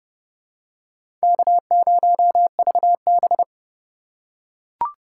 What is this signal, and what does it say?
Morse code, heard as a keyed tone, about 30 wpm, 700 Hz. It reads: K0VB
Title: USA Call Signs - 4 Characters - 8x Character Spacing - 30wpm